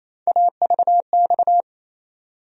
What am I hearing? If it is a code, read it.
Morse code, heard as a keyed tone, about 28 wpm, 700 Hz. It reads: AVX
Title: USA Call Sign Postfixes - 3 Letters 28wpm